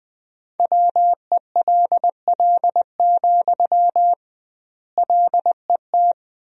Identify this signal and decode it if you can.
Morse code, heard as a keyed tone, about 20 wpm, 700 Hz. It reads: WELL, LET
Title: 2x Word-Spacing - Sentences from Top 300 Words - 20wpm